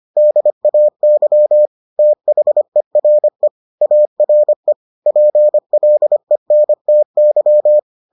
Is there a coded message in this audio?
Yes: Morse code from 0.2 to 7.8 s, heard as a keyed tone, about 25 wpm, 600 Hz. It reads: DAY THERE ARE PLENTY